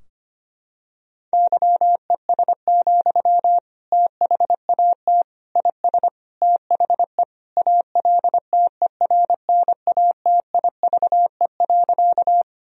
Morse code, heard as a keyed tone, about 25 wpm, 700 Hz. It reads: YES, THAT IS THE ALTERNATIVE.
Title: Sentences from Top 1000 Words - Review Part 4 - 25wpm